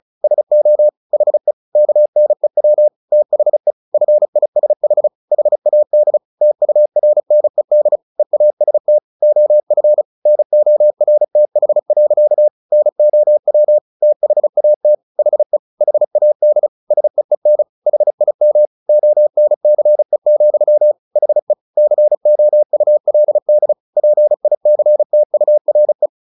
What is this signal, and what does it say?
Morse code, heard as a keyed tone, about 34 wpm, 600 Hz. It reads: SO HE KNEW THE FISH HAD TURNED EAST OF NORTH. NOW THAT HE HAD SEEN HIM ONCE, HE COULD PICTURE
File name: the old man and the sea / the old man and the sea 1 35wpm